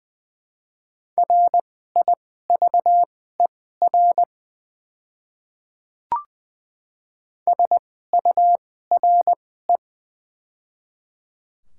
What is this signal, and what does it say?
Morse code, heard as a keyed tone, about 20 wpm, spaced as for 14 wpm, 700 Hz. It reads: RIVER SURE
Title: 2x Character Spacing - Top 300 Words - 20wpm